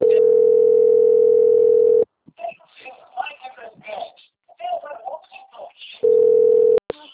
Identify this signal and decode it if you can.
Ringback tone (North America), 440 + 480 Hz, with speech.
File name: out- - - - - .41885